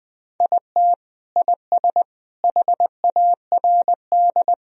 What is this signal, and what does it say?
Morse code, heard as a keyed tone, about 20 wpm, 700 Hz. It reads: IT IS HARD
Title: Sentences from Top 1000 Words - Review Part 3 - 20wpm